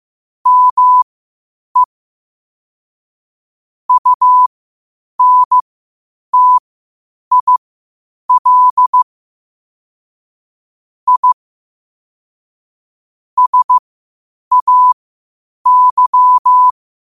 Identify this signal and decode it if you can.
Morse code, heard as a keyed tone, about 15 wpm, spaced as for 8 wpm, 1000 Hz. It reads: ME UNTIL I SAY
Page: Quotes for Fri, 15 Aug 2025 in Morse Code at 8 words per minute.